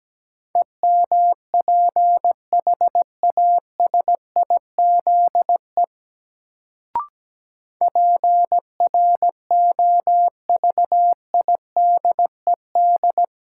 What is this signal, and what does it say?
Morse code, heard as a keyed tone, about 17 wpm, 700 Hz. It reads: EMPHASIZE PROVIDED
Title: Sentences from Top 1500 Words - New Word Drill Alt - 17wpm